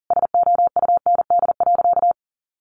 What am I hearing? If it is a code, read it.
Morse code, heard as a keyed tone, about 37 wpm, 700 Hz. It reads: SOUND.
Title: A Sound Of Thunder 40wpm